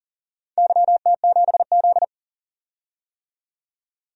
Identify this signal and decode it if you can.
Morse code, heard as a keyed tone, about 40 wpm, 700 Hz. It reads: YT7Z
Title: DXCC-100 Call Signs - 4 Characters - 40wpm